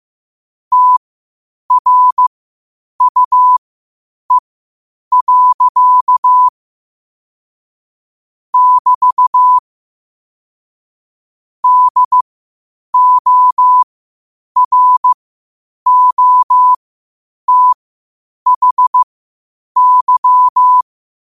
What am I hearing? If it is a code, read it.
Morse code, heard as a keyed tone, about 15 wpm, spaced as for 8 wpm, 1000 Hz. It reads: TRUE. <BT> DOROTHY
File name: qotdfiles / 8wpm